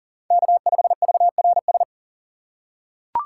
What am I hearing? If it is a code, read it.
Morse code, heard as a keyed tone, about 37 wpm, 700 Hz. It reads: K5VRS
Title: USA Call Signs - 5 Characters - 6x Character Spacing - 40wpm